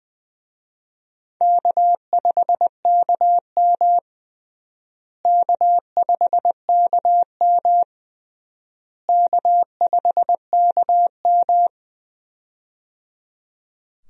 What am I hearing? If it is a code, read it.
Morse code, heard as a keyed tone, about 20 wpm, 700 Hz. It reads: K5KM K5KM K5KM
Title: USA Call Signs - 4 Characters - Repeated 3x - 20wpm